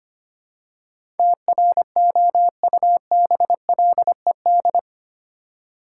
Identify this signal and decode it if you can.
Morse code, heard as a keyed tone, about 25 wpm, 700 Hz. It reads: TROUBLED
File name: Eight Letter Words - 25wpm